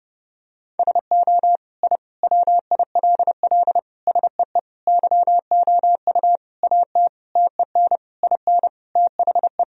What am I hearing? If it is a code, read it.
Morse code, heard as a keyed tone, about 32 wpm, 700 Hz. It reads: SO I WILL SEE YOU AT TEN IN THE
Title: Sentences from Top 1000 Words - Review Part 2 - 30wpm